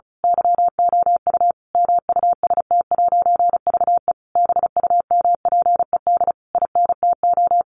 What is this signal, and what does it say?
Morse code, heard as a keyed tone, about 34 wpm, 700 Hz. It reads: YOU MUST'VE BUMPED INTO
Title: A Sound Of Thunder 35wpm